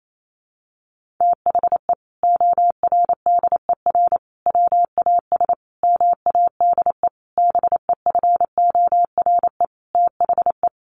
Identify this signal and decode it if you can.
Morse code, heard as a keyed tone, about 28 wpm, 700 Hz. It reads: THE ORDER WAS MADE BEFORE THE
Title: Sentences from Top 1500 Words 28wpm